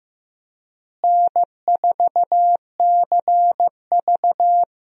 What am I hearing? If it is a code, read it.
Morse code, heard as a keyed tone, about 15 wpm, 700 Hz. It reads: N4CV
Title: USA Call Signs - 4 Characters - Repeated 3x - 15wpm